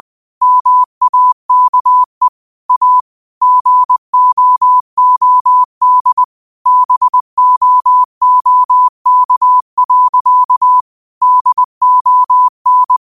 Morse code, heard as a keyed tone, about 20 wpm, 1000 Hz. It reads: MAKE A GOOD BOOK. DON